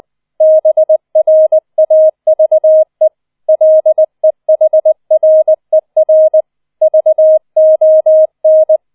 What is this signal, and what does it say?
Morse code, heard as a keyed tone, about 20 wpm, 630 Hz. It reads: BRAVE LEHRER VON